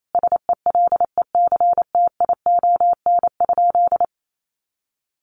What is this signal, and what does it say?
Morse code, heard as a keyed tone, about 28 wpm, 700 Hz. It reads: SELECTION?
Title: Sentences from Top 1400 Words 28wpm